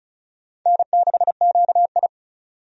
Morse code, heard as a keyed tone, about 36 wpm, 700 Hz. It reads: N6QI
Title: USA Call Signs - 4 Characters - 35wpm